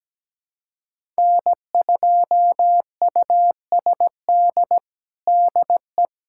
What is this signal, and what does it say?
Morse code, heard as a keyed tone, about 17 wpm, 700 Hz. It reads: N2USD DE
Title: USA Callsign DE Callsign - 5 Characters - 17wpm